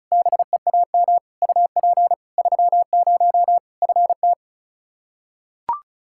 Morse code, heard as a keyed tone, about 36 wpm, 700 Hz. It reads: BEAM UP 30 FT
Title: Practice QSO - Exchange 2 - Info - 35wpm